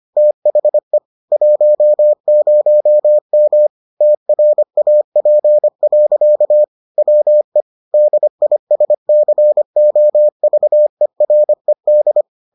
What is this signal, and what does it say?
Morse code, heard as a keyed tone, about 25 wpm, 600 Hz. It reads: THE 10M TRAP. WE DISCOVERED